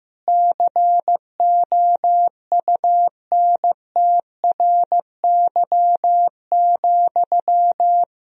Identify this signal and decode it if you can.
Morse code, heard as a keyed tone, about 15 wpm, 700 Hz. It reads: COUNTRY,
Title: Sentences from Top 1000 Words - Review Part 3 - 15wpm